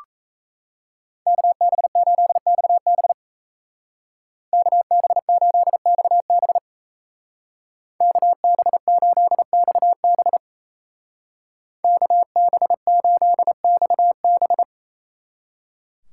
Morse code, heard as a keyed tone, about 34 wpm, 700 Hz. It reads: KB8XB KB8XB KB8XB KB8XB
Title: Speed Racing - USA Call Signs 28wpm